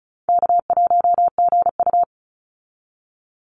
Morse code, heard as a keyed tone, about 34 wpm, 700 Hz. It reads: K1GU
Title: Call Signs - 35wpm